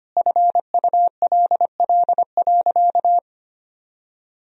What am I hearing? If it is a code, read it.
Morse code, heard as a keyed tone, about 25 wpm, 700 Hz. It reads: FULL.
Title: Sentences from Top 1000 Words - Review Part 3 - 25wpm